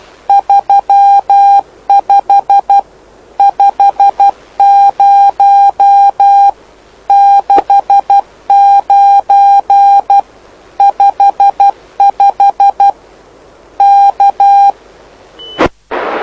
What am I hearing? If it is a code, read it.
Morse code, heard as a keyed tone, about 12 wpm, 800 Hz. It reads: 35 50 69 55 K
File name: lu4aao / lu7aa 09dic2011 exp planeador vuelo1